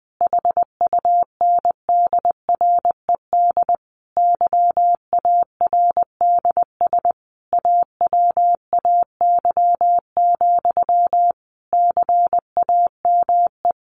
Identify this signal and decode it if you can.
Morse code, heard as a keyed tone, about 20 wpm, 700 Hz. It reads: HUNDRED YARDS AWAY, CAME
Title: A Sound Of Thunder 20wpm